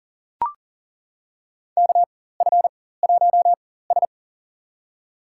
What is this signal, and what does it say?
Morse code, heard as a keyed tone, about 40 wpm, spaced as for 19 wpm, 700 Hz. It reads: KF1S
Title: USA Call Signs - 4 Characters - 4x Character Spacing - 40wpm